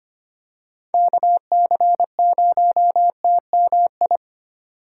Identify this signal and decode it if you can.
Morse code, heard as a keyed tone, about 25 wpm, 700 Hz. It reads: KC0TMI
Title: USA Call Signs - 6 Characters - 4x Character Spacing - 25wpm